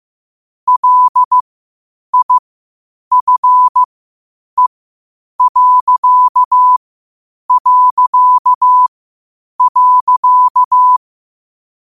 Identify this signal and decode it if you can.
Morse code, heard as a keyed tone, about 15 wpm, spaced as for 8 wpm, 1000 Hz. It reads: LIFE...
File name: qotdfiles / 8wpm